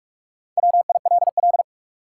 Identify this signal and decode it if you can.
Morse code, heard as a keyed tone, about 46 wpm, 700 Hz. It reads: WILL
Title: Top 180 QSO Elements 45wpm